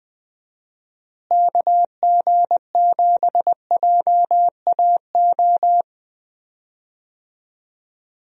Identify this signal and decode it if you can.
Morse code, heard as a keyed tone, about 20 wpm, 700 Hz. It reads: KG7JAO
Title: USA Call Signs - 6 Characters - Repeated 3x - 20wpm